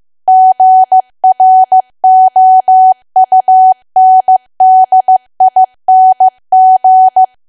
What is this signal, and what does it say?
Morse code, heard as a keyed tone, about 15 wpm, 750 Hz. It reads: GROUNDING
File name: cwa / W1AW-Level2 / 15WPM